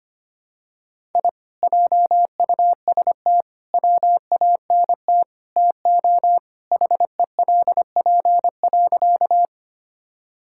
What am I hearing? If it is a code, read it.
Morse code, heard as a keyed tone, about 25 wpm, 700 Hz. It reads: I JUST WANT TO HELP.
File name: practice / 25wpm / Sentences with 4-Word Phrases - 25wpm